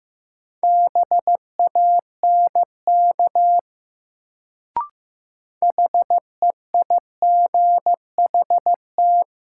Morse code, heard as a keyed tone, about 15 wpm, 700 Hz. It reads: BANK HEIGHT
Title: Sets of 5 Words - New Word Drill - 15wpm